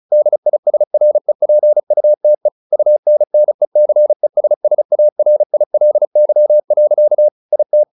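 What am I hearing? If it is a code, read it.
Morse code, heard as a keyed tone, about 35 wpm, 600 Hz. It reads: DISREPUTE UNNECESSARILY. IT
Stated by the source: Morse practice at 35 wpm from CW med Gnister, sending Ken Follett's novel Never in English.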